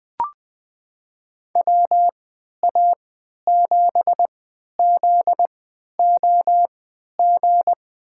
Morse code, heard as a keyed tone, about 20 wpm, spaced as for 11 wpm, 700 Hz. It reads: WA7ZOG
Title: USA Call Signs - 3x Character Spacing - 20wpm